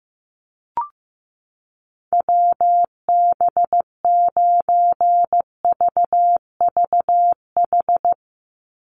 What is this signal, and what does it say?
Morse code, heard as a keyed tone, about 15 wpm, 700 Hz. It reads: WB9VVH
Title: USA Call Signs 15wpm